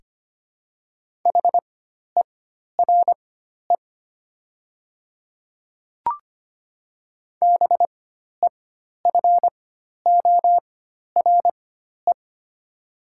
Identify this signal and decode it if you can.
Morse code, heard as a keyed tone, about 25 wpm, spaced as for 12 wpm, 700 Hz. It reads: HERE BEFORE
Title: 4x Character Spacing - Top 300 Words - 25wpm